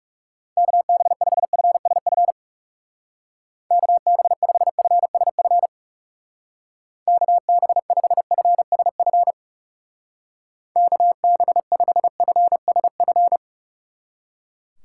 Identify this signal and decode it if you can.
Morse code, heard as a keyed tone, about 37 wpm, 700 Hz. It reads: KB5FSF KB5FSF KB5FSF KB5FSF
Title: Speed Racing - USA 2x3 Call Signs 30wpm